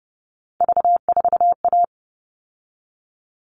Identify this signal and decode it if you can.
Morse code, heard as a keyed tone, about 28 wpm, 700 Hz. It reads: V4A
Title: All Cut Numbers - 3-Digit 30wpm